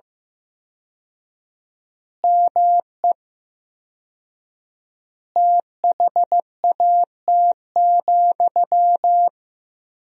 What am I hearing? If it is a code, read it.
Morse code, heard as a keyed tone, about 15 wpm, 700 Hz. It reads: ME THAT,
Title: 4x Word-Spacing - Sentences from Top 500 Words - 15wpm